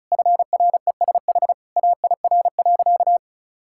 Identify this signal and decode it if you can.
Morse code, heard as a keyed tone, about 36 wpm, 700 Hz. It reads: FRESH AIR.
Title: Sentences from Top 1600 Words 35wpm